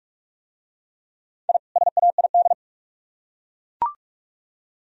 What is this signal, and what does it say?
Morse code, heard as a keyed tone, about 44 wpm, 700 Hz. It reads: I SAID